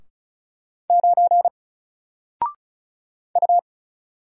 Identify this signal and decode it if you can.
Morse code, heard as a keyed tone, about 36 wpm, 700 Hz. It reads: 9 U
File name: Single-Letter-Number-35wpm